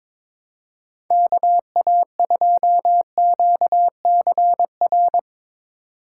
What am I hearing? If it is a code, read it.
Morse code, heard as a keyed tone, about 24 wpm, 700 Hz. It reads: KA2QCR
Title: Speed Racing - USA 2x3 Call Signs 22wpm